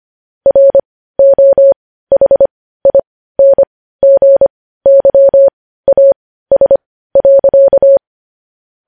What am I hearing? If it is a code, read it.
Morse code, heard as a keyed tone, about 25 wpm, spaced as for 15 wpm, 550 Hz. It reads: ROHINGYAS.
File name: podcast / News headlines-1x-25x15